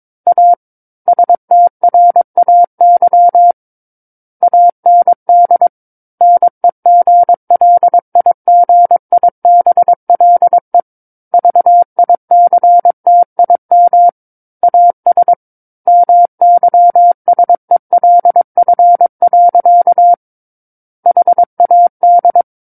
Morse code, heard as a keyed tone, about 22 wpm, 700 Hz. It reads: A STRAY AND NEGLIGIBLE VICTIM AS MYSELF. HAD
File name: Book One / 13-Chapter 13 - 22 WPM